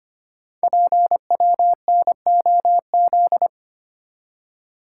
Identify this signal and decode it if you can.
Morse code, heard as a keyed tone, about 25 wpm, 700 Hz. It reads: PWNOZ
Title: Code Groups - 5 Letters - 25wpm